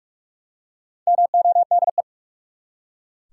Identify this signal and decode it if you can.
Morse code, heard as a keyed tone, about 43 wpm, 700 Hz. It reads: MODE
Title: Four Letter Words - 45wpm